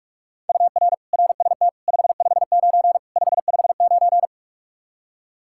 Morse code, heard as a keyed tone, about 46 wpm, 700 Hz. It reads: UR RST 559 559
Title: Practice QSO - Exchange 1 - Beginning - 45wpm